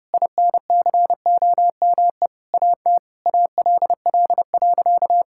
Morse code, heard as a keyed tone, about 30 wpm, 700 Hz. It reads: INCOME AT ALL.
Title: Sentences from Top 1000 Words - Review Part 5 - 30wpm